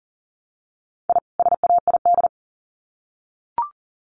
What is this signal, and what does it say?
Morse code, heard as a keyed tone, about 40 wpm, 700 Hz. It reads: I SAID
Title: Sets of 2 Words 40wpm